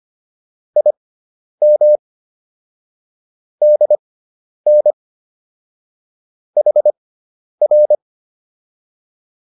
Morse code, heard as a keyed tone, about 25 wpm, spaced as for 10 wpm, 600 Hz. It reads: IM DN HR